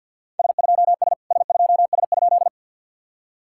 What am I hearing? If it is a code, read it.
Morse code, heard as a keyed tone, about 52 wpm, 700 Hz. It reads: S2S S2S?
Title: SOTA QSO Practice 50wpm